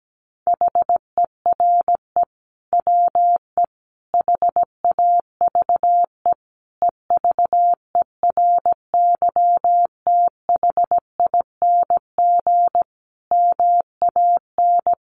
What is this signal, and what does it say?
Morse code, heard as a keyed tone, about 17 wpm, 700 Hz. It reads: HERE WE HAVE EVERYTHING MAN
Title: Sentences from Top 1100 Words 17wpm